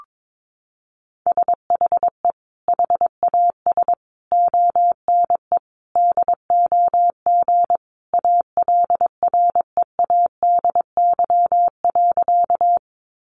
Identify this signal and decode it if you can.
Morse code, heard as a keyed tone, about 22 wpm, 700 Hz. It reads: SHE HAS ONE DOG ALREADY.
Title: Easy Phrases 22wpm